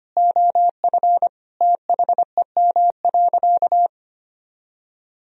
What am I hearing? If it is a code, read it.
Morse code, heard as a keyed tone, about 25 wpm, 700 Hz. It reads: OF THEM.
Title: Sentences from Top 1000 Words - Review Part 4 - 25wpm